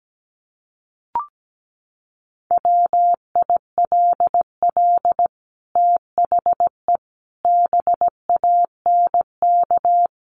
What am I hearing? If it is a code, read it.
Morse code, heard as a keyed tone, about 17 wpm, 700 Hz. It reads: WILL THE BANK
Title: Sentences from Top 1500 Words 17wpm